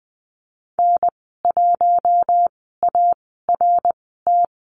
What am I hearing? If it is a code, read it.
Morse code, heard as a keyed tone, about 20 wpm, spaced as for 15 wpm, 700 Hz. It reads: N1ART
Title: 055 Lesson 10g - Letter r - Call Signs 20wpm